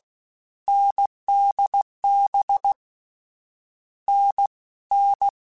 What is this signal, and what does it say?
Morse code, heard as a keyed tone, about 16 wpm, 800 Hz. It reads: NDB N N
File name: Difficult Combos / Difficult Combos-mp3s / awj1 ndb6 gz7 @15